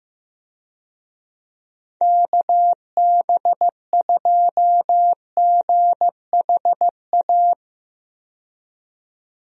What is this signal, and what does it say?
Morse code, heard as a keyed tone, about 16 wpm, 700 Hz. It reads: KB2GHA
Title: USA Call Signs - 6 Characters - 15wpm